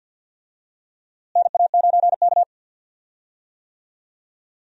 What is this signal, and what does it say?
Morse code, heard as a keyed tone, about 47 wpm, 700 Hz. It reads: NA9K
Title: USA Call Signs - 4 Characters - Repeated 2x - 50wpm